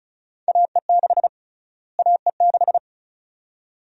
Morse code, heard as a keyed tone, about 34 wpm, 700 Hz. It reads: AE6 AE6
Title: USA Call Sign Prefixes - Repeated 2x - 35wpm